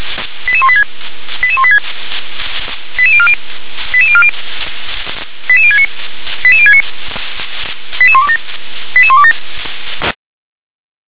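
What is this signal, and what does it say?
Five-tone selective call (ZVEI), heard as quick runs of five tones, tones of about 70 ms. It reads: 88186 88186 88040 88040 88069 88069 88127 88127
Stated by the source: ZVEI five-tone call received by SDR.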